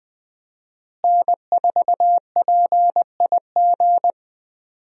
Morse code, heard as a keyed tone, about 20 wpm, 700 Hz. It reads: N4PIG
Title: USA Call Signs - 5 Characters - Repeated 2x - 20wpm